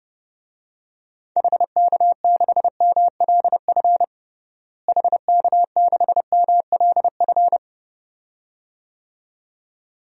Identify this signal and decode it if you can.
Morse code, heard as a keyed tone, about 30 wpm, 700 Hz. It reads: HK6MLF HK6MLF
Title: DXCC-100 Call Signs - 6 Characters - Repeated 2x - 30wpm